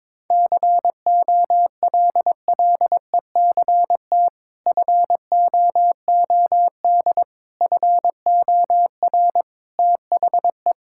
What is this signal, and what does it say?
Morse code, heard as a keyed tone, about 22 wpm, 700 Hz. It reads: COLLECT FOOD FOR THE
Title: Sentences from Top 1600 Words 22wpm